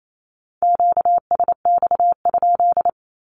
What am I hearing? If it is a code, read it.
Morse code, heard as a keyed tone, about 28 wpm, 700 Hz. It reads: QSX?
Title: Q-Codes 28wpm